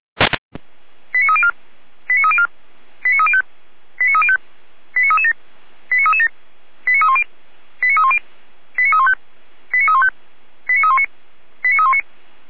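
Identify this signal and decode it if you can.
Five-tone selective call (ZVEI), heard as quick runs of five tones, tones of about 70 ms. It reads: 89304 89304 89305 89305 89337 89337 89310 89310 89315 89315 89319 89319
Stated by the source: ZVEI five-tone selective call received on SDR.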